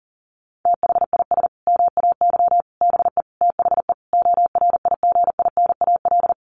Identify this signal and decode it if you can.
Morse code, heard as a keyed tone, about 40 wpm, 700 Hz. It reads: THIS MAY BE THE ORIGINAL
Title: Sentences from Top 900 Words 40wpm